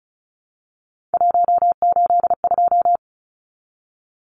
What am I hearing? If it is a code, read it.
Morse code, heard as a keyed tone, about 34 wpm, 700 Hz. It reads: 182
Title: Numbers - 3-Digit 35wpm